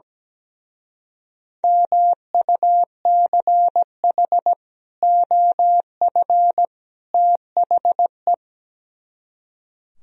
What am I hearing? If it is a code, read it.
Morse code, heard as a keyed tone, about 17 wpm, 700 Hz. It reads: MUCH OF THE
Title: Speed-Racing - Sets of 3 Words 17wpm